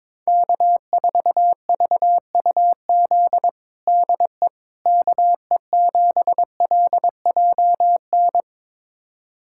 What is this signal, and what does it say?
Morse code, heard as a keyed tone, about 21 wpm, 700 Hz. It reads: K4VUZ DE KE7LJN
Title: USA Callsign DE Callsign 22wpm